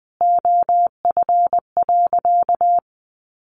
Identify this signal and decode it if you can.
Morse code, heard as a keyed tone, about 20 wpm, 700 Hz. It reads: OF.
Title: Sentences from Top 1300 Words 20wpm